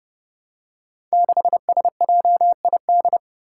Morse code, heard as a keyed tone, about 31 wpm, 700 Hz. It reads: 6SJID
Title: Five Challenging Letter-Number Combinations 30wpm